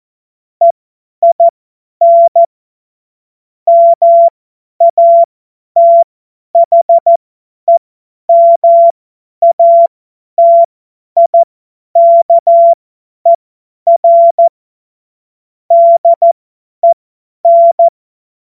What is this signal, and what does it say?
Morse code, heard as a keyed tone, about 14 wpm, spaced as for 10 wpm, 680 Hz. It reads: EIN MATHEMATIKER DEN
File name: Morsekurs / Witze / Witz 28. Dezember 2021, 50 F70